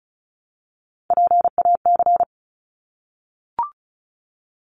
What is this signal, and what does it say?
Morse code, heard as a keyed tone, about 34 wpm, 700 Hz. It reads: PAC
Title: ARRL Sections 35wpm